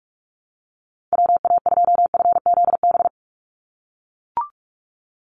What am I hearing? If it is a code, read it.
Morse code, heard as a keyed tone, about 44 wpm, 700 Hz. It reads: WA2FZB
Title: Call Signs - 45wpm